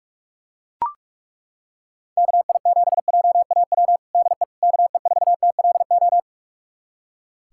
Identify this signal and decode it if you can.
Morse code, heard as a keyed tone, about 41 wpm, 700 Hz. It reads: KI7JAW DE KE4TLO
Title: USA Callsign DE Callsign 45wpm